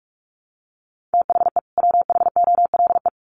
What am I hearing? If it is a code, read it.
Morse code, heard as a keyed tone, about 44 wpm, 700 Hz. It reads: THE WHOLE
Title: Sets of 2 Words 45wpm